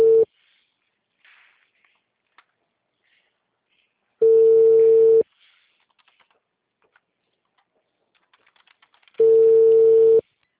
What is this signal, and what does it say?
Ringback tone (China, 450 Hz).